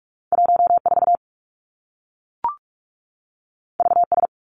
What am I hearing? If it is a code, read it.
Morse code, heard as a keyed tone, about 46 wpm, 700 Hz. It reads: 1 4 V S
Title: Two Challenging Letter-Number Combinations 45wpm